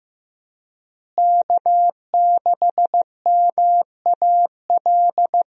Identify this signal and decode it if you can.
Morse code, heard as a keyed tone, about 15 wpm, 700 Hz. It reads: K6MAL
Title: USA Call Signs - 5 Characters - Repeated 3x - 15wpm